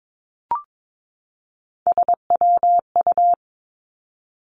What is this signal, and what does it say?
Morse code, heard as a keyed tone, about 22 wpm, 700 Hz. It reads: SWU
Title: Three Challenging Letter-Number Combinations 22wpm